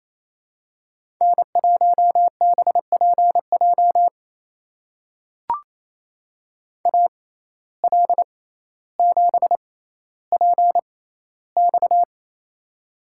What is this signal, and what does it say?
Morse code, heard as a keyed tone, about 28 wpm, spaced as for 14 wpm, 700 Hz. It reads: N1BPJ AL7PX
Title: USA Call Signs - 5 Characters - 6x Character Spacing - 28wpm